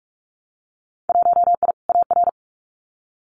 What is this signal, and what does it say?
Morse code, heard as a keyed tone, about 46 wpm, 700 Hz. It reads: 1I AR
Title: Winter Field Day Exchanges Encore 45wpm